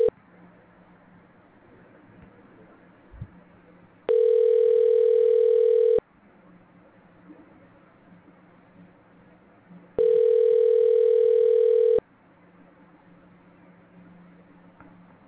Ringback tone (North America), 440 + 480 Hz.